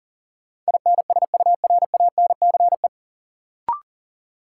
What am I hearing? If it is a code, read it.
Morse code, heard as a keyed tone, about 40 wpm, 700 Hz. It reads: INSURANCE
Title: Sentences from Top 1100 Words - New Word Drill - 40wpm